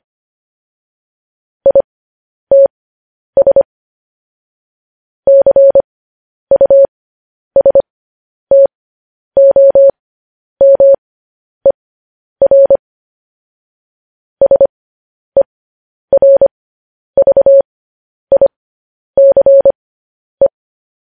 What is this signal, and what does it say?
Morse code, heard as a keyed tone, about 25 wpm, spaced as for 10 wpm, 550 Hz. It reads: ITS CUSTOMER SERVICE
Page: news-headlines-in-morse-code-at-10-wpm